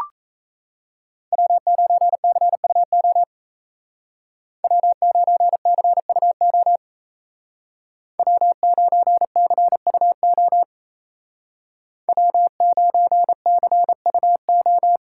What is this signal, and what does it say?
Morse code, heard as a keyed tone, about 36 wpm, 700 Hz. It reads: W9CUO W9CUO W9CUO W9CUO